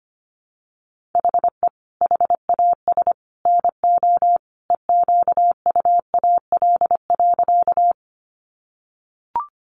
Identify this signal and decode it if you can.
Morse code, heard as a keyed tone, about 25 wpm, 700 Hz. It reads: HE HAS NO EQUAL.
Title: Sentences from Top 1400 Words 25wpm